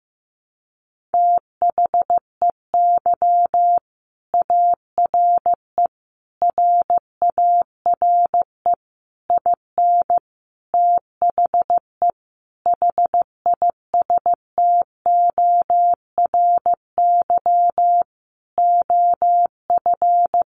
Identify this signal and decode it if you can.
Morse code, heard as a keyed tone, about 15 wpm, 700 Hz. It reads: THEY ARE RARE IN THE HISTORY OF
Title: Sentences from Top 1100 Words 15wpm